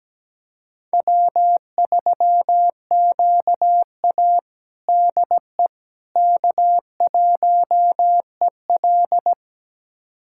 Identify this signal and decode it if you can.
Morse code, heard as a keyed tone, about 17 wpm, 700 Hz. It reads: W3QA DE K1EL
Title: USA Callsign DE Callsign - 4 Characters - 17wpm